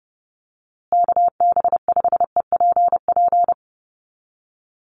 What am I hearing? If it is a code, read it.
Morse code, heard as a keyed tone, about 30 wpm, 700 Hz. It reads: KB5EPP
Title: USA Call Signs Repeated 2x 30wpm